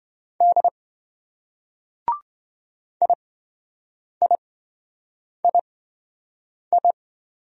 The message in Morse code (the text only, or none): D IIII